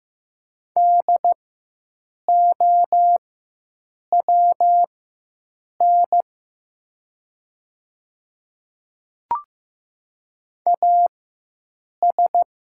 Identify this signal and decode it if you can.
Morse code, heard as a keyed tone, about 15 wpm, spaced as for 7 wpm, 700 Hz. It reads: DOWN AS
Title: 4x Character Spacing - Top 100 Words - 15wpm